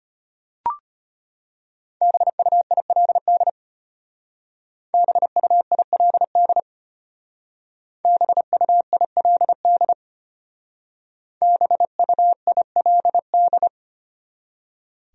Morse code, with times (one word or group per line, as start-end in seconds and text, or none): 2.0-3.5 BUILD
4.9-6.6 BUILD
8.1-9.9 BUILD
11.4-13.7 BUILD